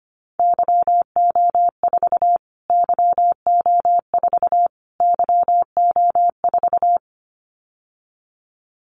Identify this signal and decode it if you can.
Morse code, heard as a keyed tone, about 25 wpm, 700 Hz. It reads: YO4 YO4 YO4
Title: Callsign Prefixes for Top 25 DXCC - Repeated 3x - 25wpm